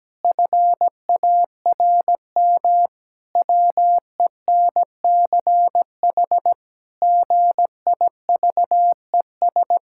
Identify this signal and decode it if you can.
Morse code, heard as a keyed tone, about 17 wpm, 700 Hz. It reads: FARM WENCH GIVES
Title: Pangram Challenge v2 - 17wpm